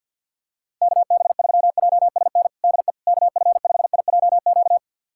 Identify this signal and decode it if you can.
Morse code, heard as a keyed tone, about 46 wpm, 690 Hz. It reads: KB3JSN DE KF5IJQ
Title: USA Callsign DE Callsign - 6 Characters - 50wpm